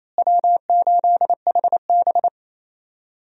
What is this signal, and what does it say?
Morse code, heard as a keyed tone, about 27 wpm, 700 Hz. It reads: W8HB
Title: USA Call Signs - 4 Characters - Repeated 3x - 28wpm